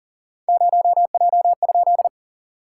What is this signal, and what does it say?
Morse code, heard as a keyed tone, about 37 wpm, 690 Hz. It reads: 0J?
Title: SOTA QSO Practice 40wpm